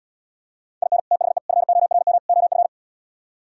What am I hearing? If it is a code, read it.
Morse code, heard as a keyed tone, about 77 wpm, 700 Hz. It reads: IN THE FORM OF